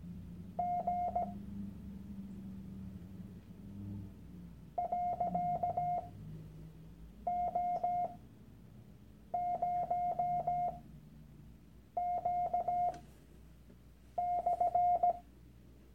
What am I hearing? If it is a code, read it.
Morse code, heard as a keyed tone, about 17 wpm, spaced as for 6 wpm, 710 Hz. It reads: G .O0Q/
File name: Workshop Dateien / CW-Online-Kurs / Lektion 001